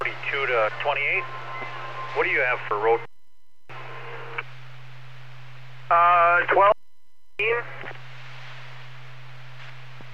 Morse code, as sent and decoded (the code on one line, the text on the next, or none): none